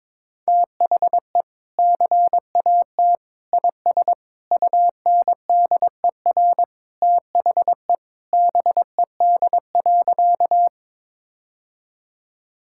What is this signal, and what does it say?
Morse code, heard as a keyed tone, about 22 wpm, 700 Hz. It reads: THE CAT IS UNDER THE BED.
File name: practice / 22wpm / Sentences with 2-Word Phrases - 22wpm